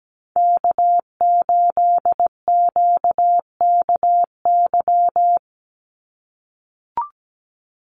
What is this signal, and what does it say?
Morse code, heard as a keyed tone, about 17 wpm, 700 Hz. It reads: K8QKY
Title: Call Signs - 17wpm